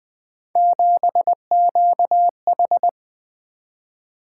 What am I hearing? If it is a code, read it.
Morse code, heard as a keyed tone, about 21 wpm, 700 Hz. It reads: 7QH
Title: Three Letter-Number Combinations - 20wpm